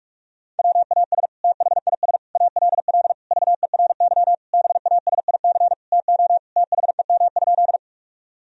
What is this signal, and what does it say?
Morse code, heard as a keyed tone, about 43 wpm, 700 Hz. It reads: WAS THIS ALL VERY BASIC TO THEM?
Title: Sentences from Top 1000 Words - Review Part 3 - 45wpm